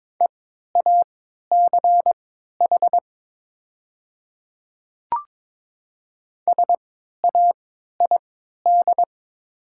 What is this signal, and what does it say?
Morse code, heard as a keyed tone, about 22 wpm, spaced as for 13 wpm, 700 Hz. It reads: EACH SAID
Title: 3x Character Spacing - Top 200 Words - 22wpm